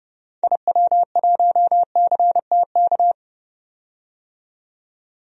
Morse code, heard as a keyed tone, about 30 wpm, 700 Hz. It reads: IW1CTK
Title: DXCC-100 Call Signs - 6 Characters - 30wpm